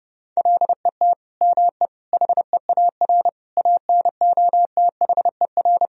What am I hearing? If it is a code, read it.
Morse code, heard as a keyed tone, about 32 wpm, 700 Hz. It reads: LET ME HEAR ANOTHER
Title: Sentences from Top 1000 Words - Review Part 2 - 30wpm